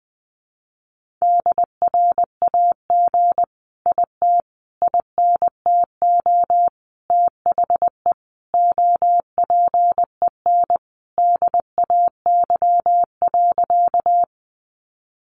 Morse code, heard as a keyed tone, about 20 wpm, 700 Hz. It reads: DRAG IT INTO THE OPEN DAY.
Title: Sentences from Top 1500 Words 20wpm